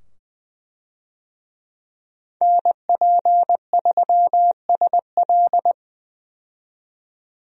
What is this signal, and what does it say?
Morse code, heard as a keyed tone, about 20 wpm, 700 Hz. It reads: NP3SL
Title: USA Call Signs - 5 Characters - 20wpm